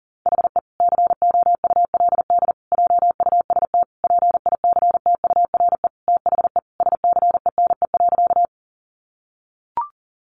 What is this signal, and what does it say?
Morse code, heard as a keyed tone, about 40 wpm, 700 Hz. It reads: HE COULD JUST PICTURE THE SCENE.
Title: Sentences from Top 1100 Words 40wpm